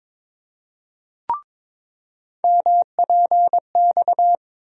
Morse code, heard as a keyed tone, about 22 wpm, 700 Hz. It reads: MPX
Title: USA Call Sign Postfixes - 3 Letters 22wpm